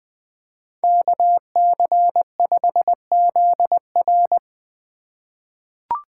Morse code, heard as a keyed tone, about 21 wpm, 700 Hz. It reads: KC5ZR
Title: USA Call Signs - 5 Characters - 4x Character Spacing - 20wpm